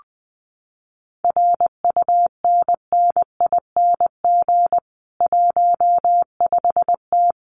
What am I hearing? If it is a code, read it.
Morse code, heard as a keyed tone, about 20 wpm, 700 Hz. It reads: RUNNING 15T
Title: Common QSO Phrases 20wpm